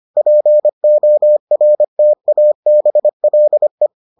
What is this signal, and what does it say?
Morse code, heard as a keyed tone, about 25 wpm, 600 Hz. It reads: PORTABLE